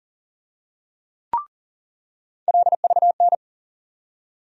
Morse code, heard as a keyed tone, about 40 wpm, 700 Hz. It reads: LVN
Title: USA Call Sign Postfixes - 3 Letters 40wpm